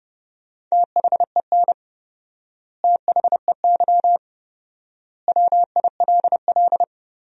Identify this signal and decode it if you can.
Morse code, heard as a keyed tone, about 31 wpm, 700 Hz. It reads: THEN THEY WILL
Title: 4x Word-Spacing - Sentences from Top 200 Words - 30wpm